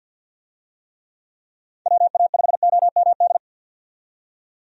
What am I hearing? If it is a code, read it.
Morse code, heard as a keyed tone, about 52 wpm, 700 Hz. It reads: WA5OMD
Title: USA Call Signs - 6 Characters - 50wpm